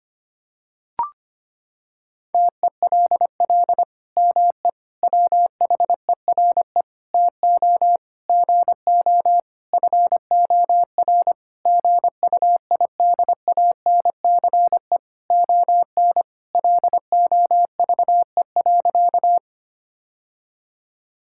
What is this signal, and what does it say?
Morse code, heard as a keyed tone, about 25 wpm, 700 Hz. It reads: TELL ME WHERE TO GO FOR GUIDANCE ON LOVE.
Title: Sentences from Top 1600 Words 25wpm